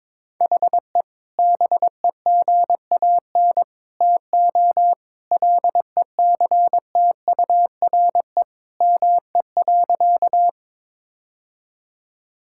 Morse code, heard as a keyed tone, about 22 wpm, 700 Hz. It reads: HE BEGAN TO LECTURE ME.
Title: Sentences from Top 1600 Words 22wpm